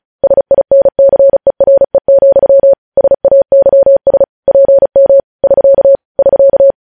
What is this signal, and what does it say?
Morse code, heard as a keyed tone, about 36 wpm, 550 Hz. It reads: SINCERE, SAYS PM <SK> <SK>
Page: News Headlines in Morse Code at 35 WPM